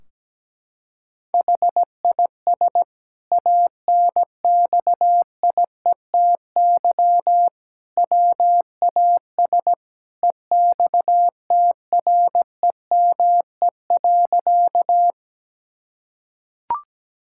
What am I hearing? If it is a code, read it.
Morse code, heard as a keyed tone, about 17 wpm, 700 Hz. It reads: HIS ANXIETY WAS EXTREME.
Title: Sentences from Top 1600 Words 17wpm